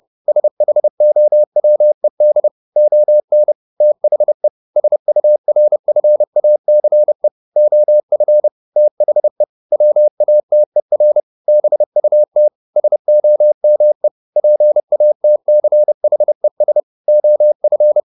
Morse code, heard as a keyed tone, about 30 wpm, 600 Hz. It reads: SHOWED ON THE SURFACE OF THE WATER BUT SOME PATCHES OF
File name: the old man and the sea / the old man and the sea 1 30wpm